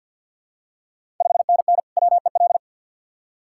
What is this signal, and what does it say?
Morse code, heard as a keyed tone, about 50 wpm, 690 Hz. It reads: 5NN WEL
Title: Ontario QSO Party Exchange 50wpm